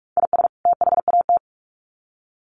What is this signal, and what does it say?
Morse code, heard as a keyed tone, about 44 wpm, 700 Hz. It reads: IS THAT
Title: Sets of 2 Words 45wpm